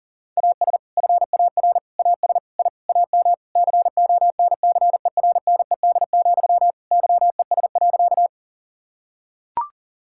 Morse code, heard as a keyed tone, about 40 wpm, 690 Hz. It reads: AS FAR AS I AM CONCERNED, YES.
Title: Sentences from Top 1000 Words - Review Part 6 - 40wpm